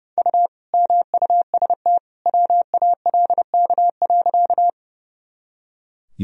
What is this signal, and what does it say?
Morse code, heard as a keyed tone, about 32 wpm, 700 Hz. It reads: U MUST WALK.
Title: Sentences from Top 300 Words and Abbreviations 30wpm